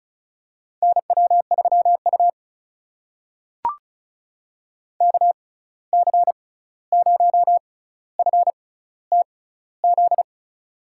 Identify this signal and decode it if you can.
Morse code, heard as a keyed tone, about 34 wpm, spaced as for 16 wpm, 700 Hz. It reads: NW3U KC0FTZ
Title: USA Call Signs - 6x Character Spacing - 35wpm